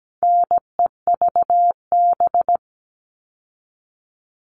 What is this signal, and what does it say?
Morse code, heard as a keyed tone, about 17 wpm, 700 Hz. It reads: NEVB
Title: All Cut Numbers - 4-Digit 17wpm